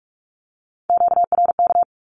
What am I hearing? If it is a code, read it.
Morse code, heard as a keyed tone, about 44 wpm, 700 Hz. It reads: QRK
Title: Q-Codes 45wpm